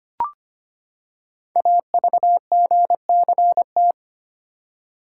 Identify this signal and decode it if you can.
Morse code, heard as a keyed tone, about 25 wpm, 700 Hz. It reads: AVGCT
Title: Code Groups - 5 Letter:Numbers - 25wpm